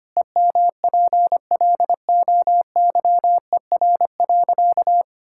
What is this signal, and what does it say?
Morse code, heard as a keyed tone, about 25 wpm, 700 Hz. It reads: EMPLOYER.
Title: Pangram Challenge v2 - 25wpm